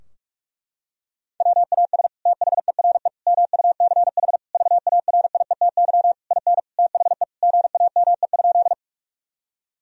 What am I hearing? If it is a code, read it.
Morse code, heard as a keyed tone, about 48 wpm, 700 Hz. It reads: WAS THERE MUCH VARIETY IN THE GAME?
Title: Sentences from Top 1000 Words - Review Part 4 - 45wpm